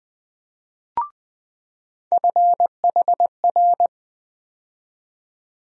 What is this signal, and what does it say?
Morse code, heard as a keyed tone, about 20 wpm, 700 Hz. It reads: FHR
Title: USA Call Sign Postfixes - 3 Letters 20wpm